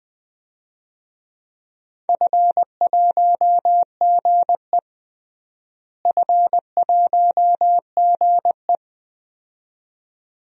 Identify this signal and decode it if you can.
Morse code, heard as a keyed tone, about 20 wpm, 700 Hz. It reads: F1GE F1GE
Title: DXCC-100 Call Signs - 4 Characters - Repeated 2x - 20wpm